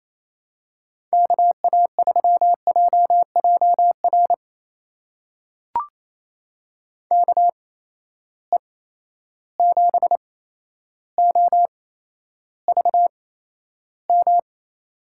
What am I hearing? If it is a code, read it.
Morse code, heard as a keyed tone, about 28 wpm, spaced as for 12 wpm, 700 Hz. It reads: KA3JJR KE7OVM
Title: USA Call Signs - 6 Characters - 8x Character Spacing - 28wpm